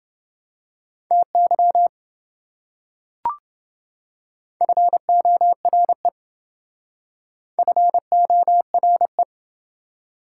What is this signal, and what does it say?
Morse code, heard as a keyed tone, about 27 wpm, 700 Hz. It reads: TY FORE FORE